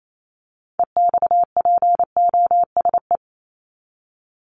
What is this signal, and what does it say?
Morse code, heard as a keyed tone, about 28 wpm, 700 Hz. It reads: EXPOSE
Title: Top 1501-1750 Words 28wpm